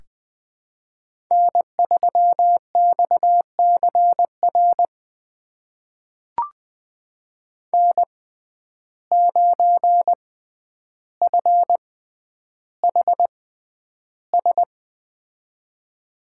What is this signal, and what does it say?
Morse code, heard as a keyed tone, about 20 wpm, spaced as for 10 wpm, 700 Hz. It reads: N3XCR N9FHS
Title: USA Call Signs - 5 Characters - 6x Character Spacing - 20wpm